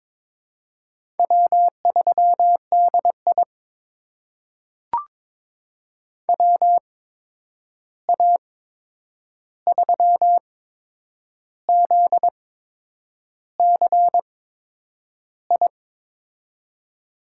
Morse code, heard as a keyed tone, about 22 wpm, spaced as for 8 wpm, 700 Hz. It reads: W3DI WA3ZCI
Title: USA Call Signs - 8x Character Spacing - 22wpm